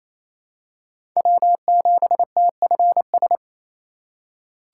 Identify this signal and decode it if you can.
Morse code, heard as a keyed tone, about 28 wpm, 700 Hz. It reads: W7TFS
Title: USA Call Signs - 5 Characters - 4x Character Spacing - 28wpm